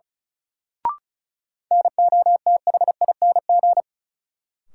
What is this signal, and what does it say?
Morse code, heard as a keyed tone, about 34 wpm, 700 Hz. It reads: NOTHING
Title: Sets of 4 Words - New Word Drill - 35wpm